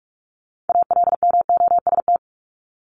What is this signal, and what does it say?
Morse code, heard as a keyed tone, about 44 wpm, 700 Hz. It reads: ALMOST
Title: Top 501-750 Words 45wpm